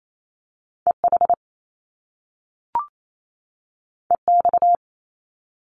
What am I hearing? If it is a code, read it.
Morse code, heard as a keyed tone, about 28 wpm, 700 Hz. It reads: EH EX
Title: Top Two Letter Combinations in Words 28wpm